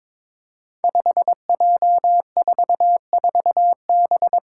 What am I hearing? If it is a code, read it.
Morse code, heard as a keyed tone, about 22 wpm, 700 Hz. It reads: 5J44B
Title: Five Challenging Letter-Number Combinations 22wpm